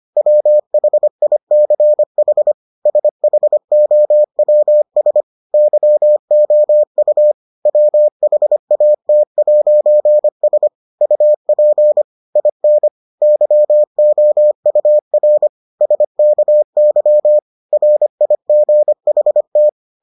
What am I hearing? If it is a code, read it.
Morse code, heard as a keyed tone, about 25 wpm, 600 Hz. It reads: WHICH SHOWS YOU WHAT'S UP IN YOUR SKY RIGHT